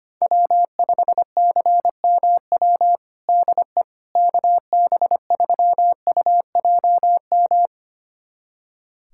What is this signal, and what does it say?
Morse code, heard as a keyed tone, about 25 wpm, 700 Hz. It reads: W5CMW DE KB3UJM
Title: USA Callsign DE Callsign 25wpm